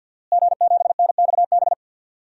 Morse code, heard as a keyed tone, about 47 wpm, 700 Hz. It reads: G7NXB
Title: DXCC-100 Call Signs - 5 Characters - Repeated 2x - 50wpm